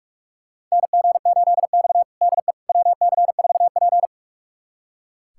Morse code, heard as a keyed tone, about 41 wpm, 700 Hz. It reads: NG8X DE WC4P
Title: USA Callsign DE Callsign - 4 Characters - 45wpm